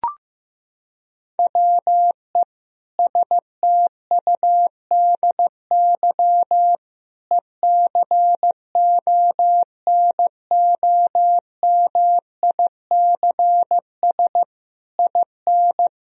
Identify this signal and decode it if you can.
Morse code, heard as a keyed tone, about 15 wpm, 700 Hz. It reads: WE STUDY ECONOMICS IN
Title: Sentences from Top 1000 Words - Review Part 3 - 15wpm